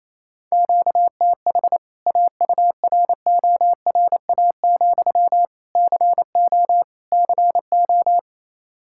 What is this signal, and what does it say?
Morse code, heard as a keyed tone, about 28 wpm, 700 Hz. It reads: QTH AURORA, CO CO
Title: K1USN SST QSO Practice 28wpm